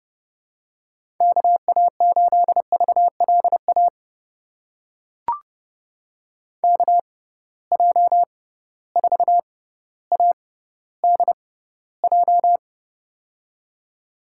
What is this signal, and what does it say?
Morse code, heard as a keyed tone, about 30 wpm, spaced as for 15 wpm, 700 Hz. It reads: KA8VLA KJ4ADJ
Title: USA Call Signs - 6 Characters - 6x Character Spacing - 30wpm